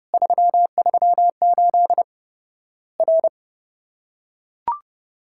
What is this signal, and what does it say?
Morse code, heard as a keyed tone, about 30 wpm, 700 Hz. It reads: 338 R
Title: Worked All Europe DX Contest - QTC Exchange - 30wpm